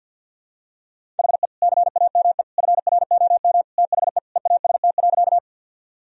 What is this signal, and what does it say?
Morse code, heard as a keyed tone, about 50 wpm, 700 Hz. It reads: HE CAME FROM THE EAST.
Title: Sentences from Top 1000 Words - Review Part 3 - 50wpm